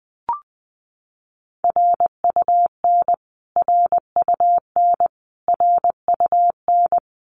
Morse code, heard as a keyed tone, about 20 wpm, 700 Hz. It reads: RUN RUN RUN
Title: Top 500 Words Repeated 20wpm